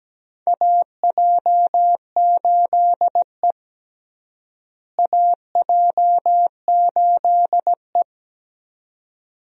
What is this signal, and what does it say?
Morse code, heard as a keyed tone, about 17 wpm, 700 Hz. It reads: AJ8E AJ8E
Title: USA Call Signs - 4 Characters - Repeated 2x - 17wpm